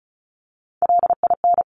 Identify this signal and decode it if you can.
Morse code, heard as a keyed tone, about 36 wpm, 700 Hz. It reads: LIN
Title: Top Three Letter Combinations in Words 35wpm